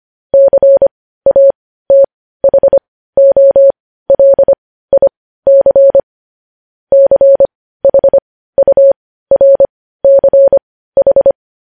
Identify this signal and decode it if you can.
Morse code, heard as a keyed tone, about 25 wpm, spaced as for 15 wpm, 550 Hz. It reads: CATHOLIC CHURCH
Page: News Headlines in Morse Code at 15 WPM